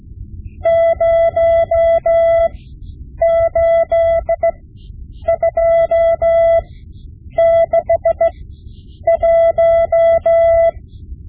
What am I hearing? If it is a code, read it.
Morse code, heard as a keyed tone, about 16 wpm, spaced as for 8 wpm, 670 Hz. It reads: 08261
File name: Morse Lessons / Morse Lessons / Tape-5B